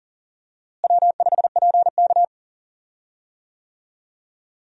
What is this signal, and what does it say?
Morse code, heard as a keyed tone, about 40 wpm, 700 Hz. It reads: W5PK
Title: USA Call Signs - 4 Characters - Repeated 3x - 40wpm